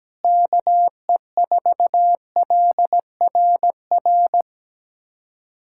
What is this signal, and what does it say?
Morse code, heard as a keyed tone, about 17 wpm, 700 Hz. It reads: KE4LRR
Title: USA Callsign DE Callsign 17wpm